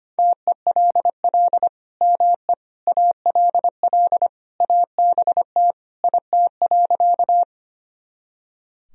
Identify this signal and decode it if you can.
Morse code, heard as a keyed tone, about 25 wpm, 700 Hz. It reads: TELL ME ALL ABT IT.
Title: Sentences from Top 200 Words and Abbreviations 25wpm